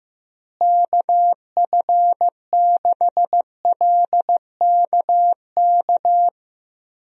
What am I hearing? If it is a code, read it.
Morse code, heard as a keyed tone, about 15 wpm, 700 Hz. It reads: KF6LKK
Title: USA Call Signs - 6 Characters - Repeated 2x - 15wpm